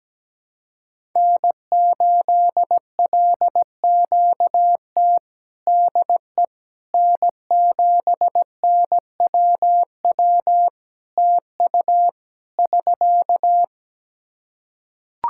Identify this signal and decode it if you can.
Morse code, heard as a keyed tone, about 17 wpm, 700 Hz. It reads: N8LQT DE N7NWW TU <SK>
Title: Practice QSO - Exchange 4 - Ending - 17wpm